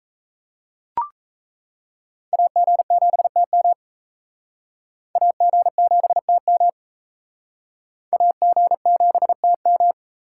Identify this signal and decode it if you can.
Morse code, heard as a keyed tone, about 34 wpm, 700 Hz. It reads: AG7TM AG7TM AG7TM
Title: Speed Racing - USA 2x2 Call Signs 28wpm